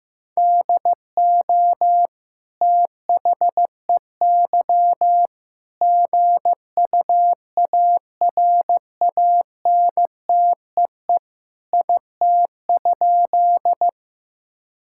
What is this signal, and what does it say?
Morse code, heard as a keyed tone, about 16 wpm, 700 Hz. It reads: DO THEY GUARANTEE IT?
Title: Sentences from Top 1600 Words 15wpm